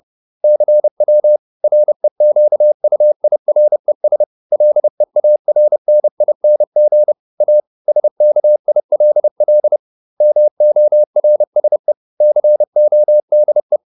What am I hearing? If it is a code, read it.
Morse code, heard as a keyed tone, about 30 wpm, 600 Hz. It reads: CW REQUIRES LEARNING A SKILL MORSE CODE